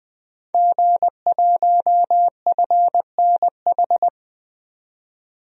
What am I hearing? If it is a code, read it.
Morse code, heard as a keyed tone, about 20 wpm, 700 Hz. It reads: G1FNH
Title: DXCC-100 Call Signs - 5 Characters - 20wpm